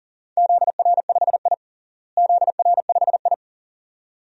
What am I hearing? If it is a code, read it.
Morse code, heard as a keyed tone, about 40 wpm, 700 Hz. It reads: ZR5I ZR5I
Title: DXCC-100 Call Signs - 4 Characters - Repeated 2x - 40wpm